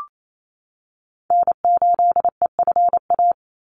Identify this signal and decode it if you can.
Morse code, heard as a keyed tone, about 28 wpm, 700 Hz. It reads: N8EFA
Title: USA Call Signs 28wpm